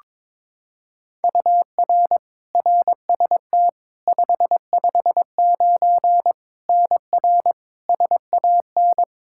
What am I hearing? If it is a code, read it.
Morse code, heard as a keyed tone, about 22 wpm, 700 Hz. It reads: UR RST 559 NR SAN
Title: Practice QSO - Exchange 1 - Beginning - 22wpm